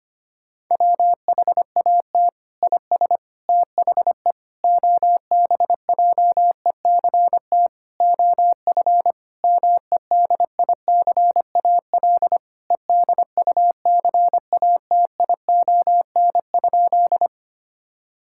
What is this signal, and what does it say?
Morse code, heard as a keyed tone, about 25 wpm, 700 Hz. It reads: WHAT IS THE OBJECT OF MEDICAL EDUCATION?
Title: Sentences from Top 1000 Words - Review Part 6 - 25wpm